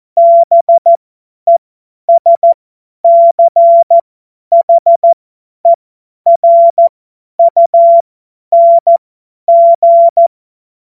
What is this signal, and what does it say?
Morse code, heard as a keyed tone, about 14 wpm, spaced as for 10 wpm, 680 Hz. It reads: BESCHERUNG
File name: Morsekurs / Witze / Witz 18. Juli 2023 50 F70